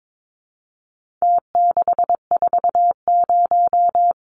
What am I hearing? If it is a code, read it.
Morse code, heard as a keyed tone, about 22 wpm, 700 Hz. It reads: T640
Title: Numbers - Contest Style - 4-Digit 22wpm